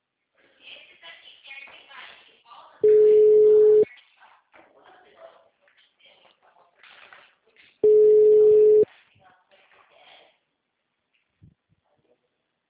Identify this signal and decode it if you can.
Ringback tone (Europe, 425 Hz), with speech.
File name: out- -2002- - - .0